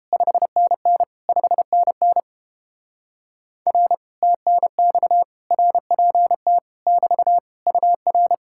Morse code, heard as a keyed tone, about 29 wpm, 700 Hz. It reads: 5NN 5NN R TNX RPT <BT> UR